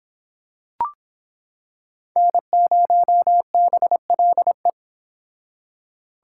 Morse code, heard as a keyed tone, about 25 wpm, 700 Hz. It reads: N0BLE